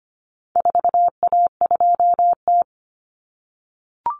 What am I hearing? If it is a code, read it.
Morse code, heard as a keyed tone, about 25 wpm, 700 Hz. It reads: 4A2T
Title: Common Cut Numbers - 4-Digit 25wpm